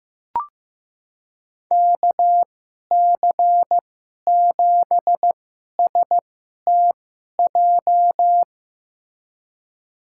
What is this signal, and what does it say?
Morse code, heard as a keyed tone, about 15 wpm, spaced as for 11 wpm, 700 Hz. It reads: KC7STJ
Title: USA Call Signs - 6 Characters - 2x Character Spacing - 15wpm